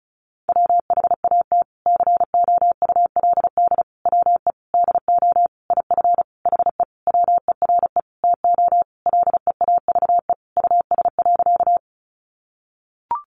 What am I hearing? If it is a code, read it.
Morse code, heard as a keyed tone, about 34 wpm, 700 Hz. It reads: WHAT COULD WE DO IF HE WERE TO LEAVE US.
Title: Sentences from Top 400 Words 35wpm